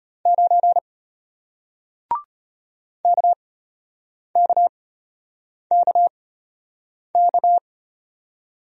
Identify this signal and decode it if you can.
Morse code, heard as a keyed tone, about 33 wpm, spaced as for 8 wpm, 700 Hz. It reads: 9 KKKK